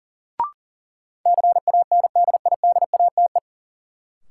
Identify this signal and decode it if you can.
Morse code, heard as a keyed tone, about 40 wpm, 700 Hz. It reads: CANDIDATE
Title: Sentences from Top 1600 Words - New Word Drill Alt - 40wpm